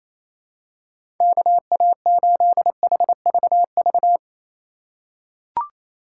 Morse code, heard as a keyed tone, about 28 wpm, 700 Hz. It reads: KA8HVV
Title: USA Call Signs - 3x Character Spacing - 28wpm